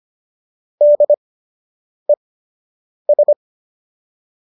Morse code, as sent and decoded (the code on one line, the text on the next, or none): -.. ...
DES